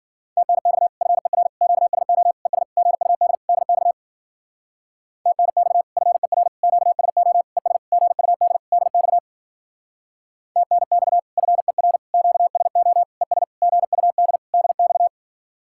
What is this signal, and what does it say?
Morse code, heard as a keyed tone, about 52 wpm, 700 Hz. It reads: TNX FER QSO ES GUD DX TNX FER QSO ES GUD DX TNX FER QSO ES GUD DX